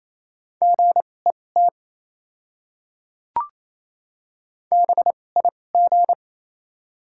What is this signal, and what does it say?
Morse code, heard as a keyed tone, about 28 wpm, spaced as for 20 wpm, 700 Hz. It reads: GET BIG
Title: 2x Character Spacing - Top 200 Words - 28wpm